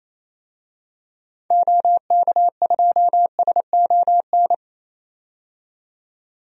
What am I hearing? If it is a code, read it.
Morse code, heard as a keyed tone, about 28 wpm, 700 Hz. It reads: OK2SON
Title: DXCC-100 Call Signs - 6 Characters - 28wpm